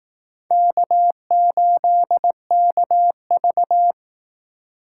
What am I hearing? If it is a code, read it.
Morse code, heard as a keyed tone, about 19 wpm, 700 Hz. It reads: K8KV
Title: Speed Racing - USA 1x2 Call Signs 15wpm